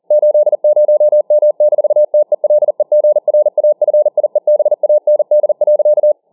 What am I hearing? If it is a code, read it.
Morse code, heard as a keyed tone, about 40 wpm, 600 Hz. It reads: 80M-TELEGRAFIEBAND.